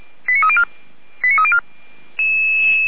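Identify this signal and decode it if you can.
Five-tone selective call (ZVEI), heard as quick runs of five tones, tones of about 70 ms. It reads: 89304 89304
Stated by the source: ZVEI five-tone selective call received by SDR, fire brigade Feuerwehr Orlamünde.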